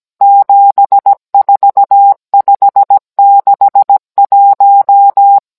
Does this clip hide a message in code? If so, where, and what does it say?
Yes: Morse code from 0.2 to 5.4 s, heard as a keyed tone, about 17 wpm, 800 Hz. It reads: 74561